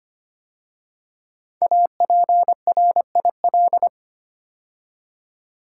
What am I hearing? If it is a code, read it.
Morse code, heard as a keyed tone, about 25 wpm, 700 Hz. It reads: APRIL
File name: 5 Letter Words - First 3 Letters Given - 25wpm